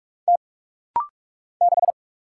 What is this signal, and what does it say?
Morse code, heard as a keyed tone, about 44 wpm, 700 Hz. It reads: T 6
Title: Single Letter-Number Alt - Rapid-Fire 45wpm